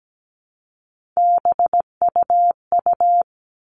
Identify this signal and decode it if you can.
Morse code, heard as a keyed tone, about 17 wpm, 700 Hz. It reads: BUU
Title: All Cut Numbers - 3-Digit 17wpm